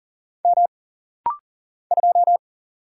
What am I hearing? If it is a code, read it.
Morse code, heard as a keyed tone, about 40 wpm, 700 Hz. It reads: M 2
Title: Single Letter-Number Alt - Rapid-Fire 40wpm